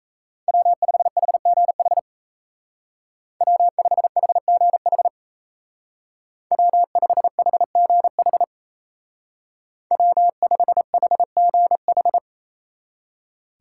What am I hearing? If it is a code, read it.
Morse code, heard as a keyed tone, about 33 wpm, 700 Hz. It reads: W5HGH W5HGH W5HGH W5HGH
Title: Speed Racing - USA 1x3 Call Signs 28wpm